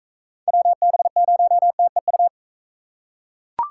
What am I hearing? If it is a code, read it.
Morse code, heard as a keyed tone, about 43 wpm, 700 Hz. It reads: WB0TEU